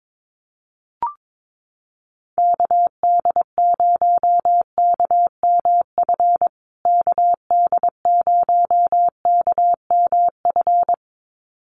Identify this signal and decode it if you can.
Morse code, heard as a keyed tone, about 22 wpm, 700 Hz. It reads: KD0KMF KD0KMF
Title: USA Call Signs Repeated 2x 22wpm